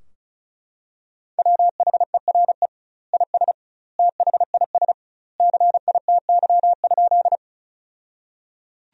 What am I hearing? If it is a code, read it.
Morse code, heard as a keyed tone, about 36 wpm, 700 Hz. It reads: WHERE IS THIS CITY?
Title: 2x Word-Spacing - Sentences from Top 300 Words - 35wpm